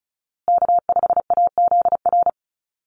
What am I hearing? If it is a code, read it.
Morse code, heard as a keyed tone, about 34 wpm, 700 Hz. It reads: K5AZR
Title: Call Signs Repeated 2x - 35wpm